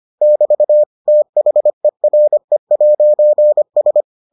Morse code, heard as a keyed tone, about 25 wpm, 600 Hz. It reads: <BT> THERE'S